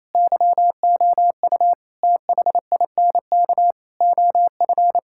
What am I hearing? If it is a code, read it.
Morse code, heard as a keyed tone, about 28 wpm, 700 Hz. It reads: YOU THINK OF
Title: Sentences from Top 1600 Words 28wpm